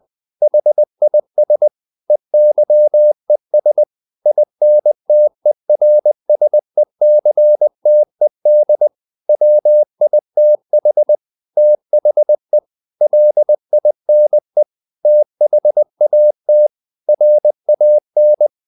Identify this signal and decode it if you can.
Morse code, heard as a keyed tone, about 20 wpm, 600 Hz. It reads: HIS EYES INTERSECTED WITH THE LINE THAT RAN
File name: the old man and the sea part 2 20wpm